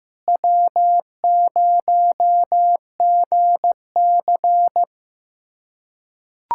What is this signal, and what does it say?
Morse code, heard as a keyed tone, about 15 wpm, 700 Hz. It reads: W0GC
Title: USA Callsign DE Callsign - 4 Characters - 15wpm